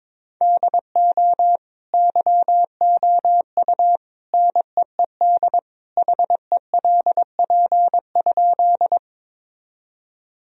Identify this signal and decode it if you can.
Morse code, heard as a keyed tone, about 22 wpm, 700 Hz. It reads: DO YOU NEED HELP?
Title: Sentences from Top 1000 Words - Review Part 1 - 22wpm